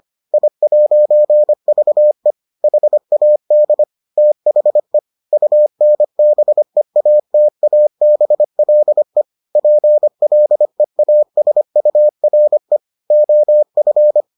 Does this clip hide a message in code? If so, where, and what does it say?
Yes: Morse code from 0.3 to 14.2 s, heard as a keyed tone, about 25 wpm, 600 Hz. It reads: I'VE HAD THE UNBEATABLE PLEASURE OF